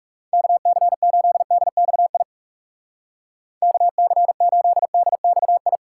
Morse code, heard as a keyed tone, about 37 wpm, 700 Hz. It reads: KC8DXI KC8DXI
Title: Speed Racing - USA 2x3 Call Signs 30wpm